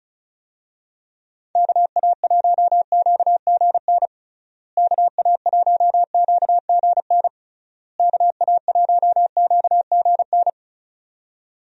Morse code, heard as a keyed tone, about 36 wpm, 700 Hz. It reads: KA1QGN KA1QGN KA1QGN
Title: USA Call Signs - 6 Characters - Repeated 3x - 35wpm